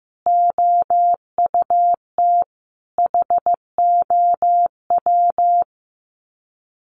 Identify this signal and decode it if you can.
Morse code, heard as a keyed tone, about 14 wpm, 700 Hz. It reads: OUT HOW
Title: Sets of 5 Words Encore 15wpm